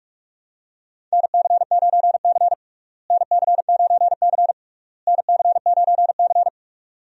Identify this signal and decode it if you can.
Morse code, heard as a keyed tone, about 46 wpm, 700 Hz. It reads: NC9C NC9C NC9C
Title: USA Call Signs - 4 Characters - Repeated 3x - 45wpm